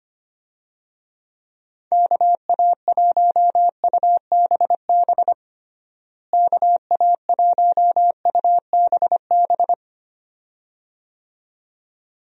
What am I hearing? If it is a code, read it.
Morse code, heard as a keyed tone, about 25 wpm, 700 Hz. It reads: KA1UBB KA1UBB
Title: USA Call Signs - 6 Characters - Repeated 2x - 25wpm